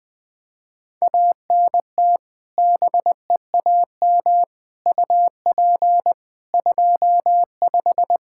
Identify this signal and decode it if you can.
Morse code, heard as a keyed tone, about 20 wpm, 700 Hz. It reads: ANT BEAM UP 25
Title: K1USN SST QSO Practice 20wpm